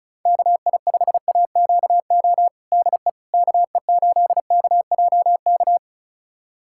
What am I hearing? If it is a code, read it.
Morse code, heard as a keyed tone, about 36 wpm, 700 Hz. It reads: KI5AQO DE KE8KJK
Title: USA Callsign DE Callsign 35wpm